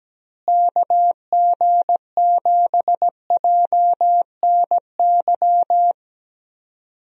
Morse code, heard as a keyed tone, about 17 wpm, 700 Hz. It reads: KG7JNY
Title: USA Call Signs - 6 Characters - Repeated 2x - 17wpm